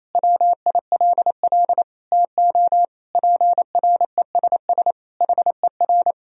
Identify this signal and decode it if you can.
Morse code, heard as a keyed tone, about 28 wpm, 700 Hz. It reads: WILL TO PRESS HER
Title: Sentences from Top 1000 Words - Review Part 1 - 28wpm